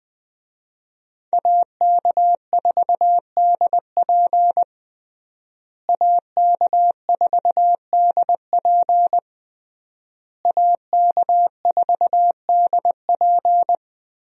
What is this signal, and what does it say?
Morse code, heard as a keyed tone, about 20 wpm, 700 Hz. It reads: AK4DP AK4DP AK4DP
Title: USA Call Signs - 5 Characters - Repeated 3x - 20wpm